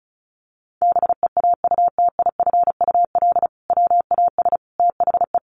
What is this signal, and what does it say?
Morse code, heard as a keyed tone, about 34 wpm, 700 Hz. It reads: BEAUTIFUL WAS THE
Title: Sentences from Top 1100 Words 35wpm